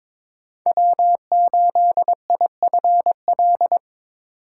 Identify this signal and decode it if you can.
Morse code, heard as a keyed tone, about 26 wpm, 700 Hz. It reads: W8IFL
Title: Speed Racing - USA 1x3 Call Signs 22wpm